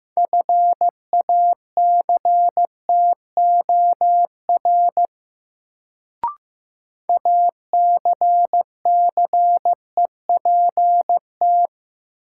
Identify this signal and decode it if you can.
Morse code, heard as a keyed tone, about 15 wpm, 700 Hz. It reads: FACTOR ACCEPT
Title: Sentences from Top 1300 Words - New Word Drill - 15wpm